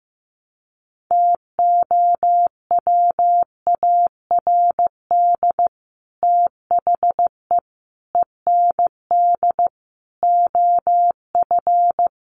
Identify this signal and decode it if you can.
Morse code, heard as a keyed tone, about 15 wpm, 700 Hz. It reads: TOWARD THE END OF
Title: Sets of 5 Words 15wpm